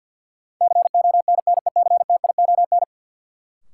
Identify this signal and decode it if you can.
Morse code, heard as a keyed tone, about 50 wpm, 700 Hz. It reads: CONNECTION
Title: Sentences from Top 1300 Words - New Word Drill Alt - 50wpm